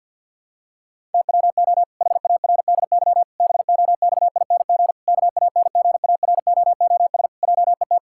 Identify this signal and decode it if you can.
Morse code, heard as a keyed tone, about 50 wpm, 700 Hz. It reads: TWO HARDY BOXING KANGAROOS JET
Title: Pangram Challenge v2 - 50wpm